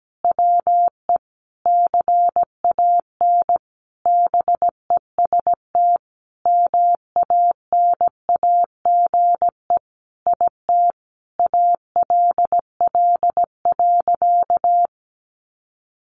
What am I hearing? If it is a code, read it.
Morse code, heard as a keyed tone, about 17 wpm, 700 Hz. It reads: WE CAN BEST MANAGE IT ALL.
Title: Sentences from Top 1400 Words 17wpm